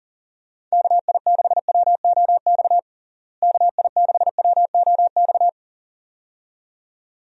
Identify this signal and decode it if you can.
Morse code, heard as a keyed tone, about 37 wpm, 700 Hz. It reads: KI6WOX KI6WOX
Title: USA Call Signs - 6 Characters - Repeated 2x - 40wpm